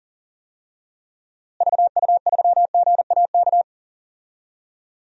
Morse code, heard as a keyed tone, about 40 wpm, 700 Hz. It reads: VU3GAK
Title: DXCC-100 Call Signs - 6 Characters - Repeated 2x - 40wpm